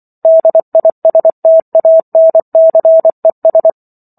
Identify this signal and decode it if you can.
Morse code, heard as a keyed tone, about 24 wpm, 650 Hz. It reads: DISTANCES